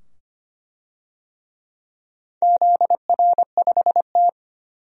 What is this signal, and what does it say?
Morse code, heard as a keyed tone, about 25 wpm, 700 Hz. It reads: ZR5T
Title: DXCC-100 Call Signs - 4 Characters - 25wpm